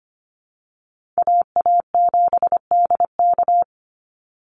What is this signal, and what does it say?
Morse code, heard as a keyed tone, about 25 wpm, 700 Hz. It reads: AA7DK
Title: NA SOTA Chaser Callsigns 25wpm